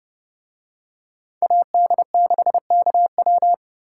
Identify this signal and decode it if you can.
Morse code, heard as a keyed tone, about 29 wpm, 700 Hz. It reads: AD6KW